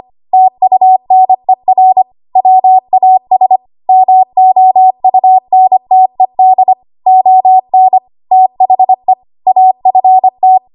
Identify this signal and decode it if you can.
Morse code, heard as a keyed tone, about 25 wpm, 750 Hz. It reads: TUNER WAS MOUNTED ON THE AFT
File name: Archive / 25 WPM / 25WPM